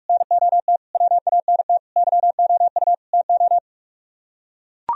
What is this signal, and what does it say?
Morse code, heard as a keyed tone, about 49 wpm, 700 Hz. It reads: NOT WANT YOU TO